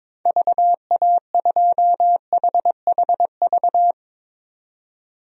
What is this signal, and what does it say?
Morse code, heard as a keyed tone, about 22 wpm, 700 Hz. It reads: VA2HHV
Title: DXCC-100 Call Signs - 6 Characters - 22wpm